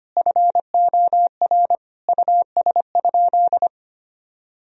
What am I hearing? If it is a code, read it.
Morse code, heard as a keyed tone, about 25 wpm, 700 Hz. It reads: FOR US?
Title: Sentences from Top 1000 Words - Review Part 3 - 25wpm